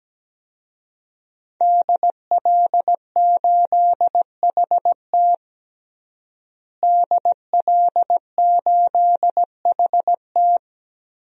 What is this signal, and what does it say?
Morse code, heard as a keyed tone, about 17 wpm, 700 Hz. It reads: DL8HT DL8HT
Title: DXCC-100 Call Signs - 5 Characters - Repeated 3x - 17wpm